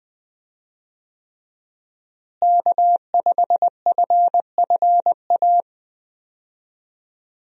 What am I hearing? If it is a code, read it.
Morse code, heard as a keyed tone, about 20 wpm, 700 Hz. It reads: K5FFA
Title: USA Call Signs - 5 Characters - Repeated 3x - 20wpm